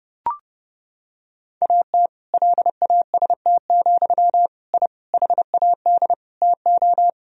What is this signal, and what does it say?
Morse code, heard as a keyed tone, about 30 wpm, 700 Hz. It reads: AT LAST, I HAD TO
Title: Sentences from Top 1000 Words - Review Part 2 - 30wpm